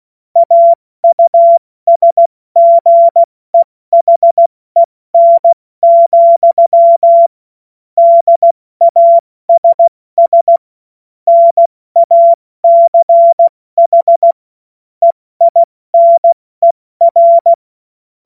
Morse code, heard as a keyed tone, about 16 wpm, 680 Hz. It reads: AUSGEHEN, DASS NACH EINER
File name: Morsekurs / Witze / Witz 21. Juni 2022 70 F80